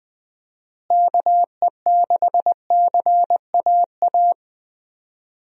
Morse code, heard as a keyed tone, about 20 wpm, 700 Hz. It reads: KE6CAA
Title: USA Call Signs - 6 Characters - Repeated 3x - 20wpm